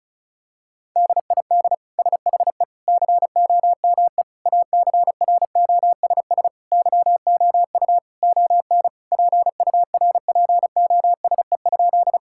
Morse code, heard as a keyed tone, about 34 wpm, 700 Hz. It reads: DID SHE COME ACROSS YOU ON PURPOSE?
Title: Sentences from Top 1000 Words - Review Part 5 - 35wpm